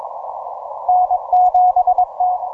Morse code, heard as a keyed tone, about 22 wpm, 730 Hz. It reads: N7T